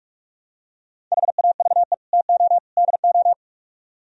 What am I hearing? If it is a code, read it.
Morse code, heard as a keyed tone, about 45 wpm, 700 Hz. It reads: HAVE TO DO